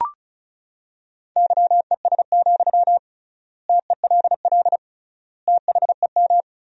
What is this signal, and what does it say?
Morse code, heard as a keyed tone, about 34 wpm, 700 Hz. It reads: YES, TELL THEM
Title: 3x Word-Spacing - Sentences from Top 500 Words - 35wpm